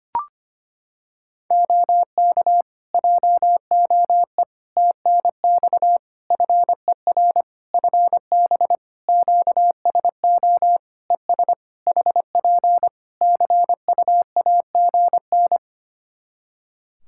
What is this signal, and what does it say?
Morse code, heard as a keyed tone, about 25 wpm, 700 Hz. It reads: OK JOE TNX FER FB QSO ES HP CUAGN
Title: K1USN SST QSO Practice 25wpm